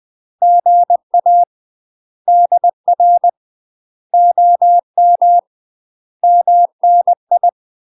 Morse code, heard as a keyed tone, about 20 wpm, 700 Hz. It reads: GA DR OM MNI
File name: UK MASTER QSOs TRAINING / 20WPM / UK MASTER QSOs 0038